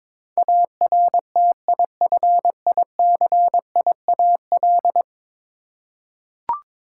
Morse code, heard as a keyed tone, about 22 wpm, 700 Hz. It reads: ARTIFICIAL
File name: practice / 22wpm / 10 Letter Words - First 3 Letters Given - 22wpm